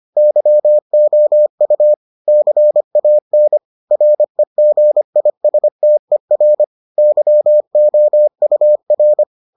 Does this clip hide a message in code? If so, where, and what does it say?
Yes: Morse code from 0.2 to 9.2 s, heard as a keyed tone, about 25 wpm, 600 Hz. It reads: YOU CAN REGISTER YOUR